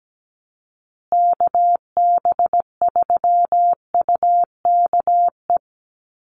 Morse code, heard as a keyed tone, about 17 wpm, 700 Hz. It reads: KB3UKE
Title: USA Call Signs Repeated 2x 17wpm